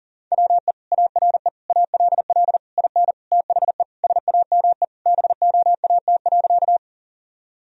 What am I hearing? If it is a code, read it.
Morse code, heard as a keyed tone, about 40 wpm, 700 Hz. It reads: WE ARE ALL IN THE SAME BOAT.
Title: Sentences from Top 1000 Words - Review Part 3 - 40wpm